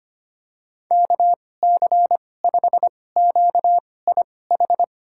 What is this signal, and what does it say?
Morse code, heard as a keyed tone, about 25 wpm, spaced as for 18 wpm, 700 Hz. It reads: KC5QIH
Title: USA Call Signs - 6 Characters - 2x Character Spacing - 25wpm